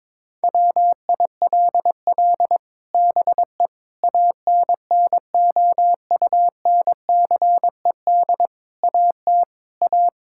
Morse code, heard as a keyed tone, about 22 wpm, 700 Hz. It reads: WILL BE ANNOUNCED AT A
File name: practice / 22wpm / Sets of 7 Words - 22wpm